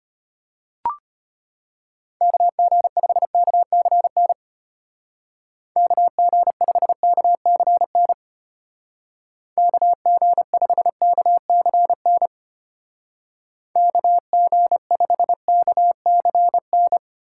Morse code, heard as a keyed tone, about 31 wpm, 700 Hz. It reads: KG5KCN KG5KCN KG5KCN KG5KCN